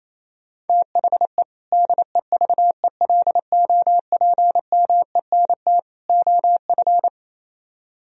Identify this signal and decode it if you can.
Morse code, heard as a keyed tone, about 33 wpm, 700 Hz. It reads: THE DEVELOPMENT OF
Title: Speed-Racing - Sets of 3 Words 28wpm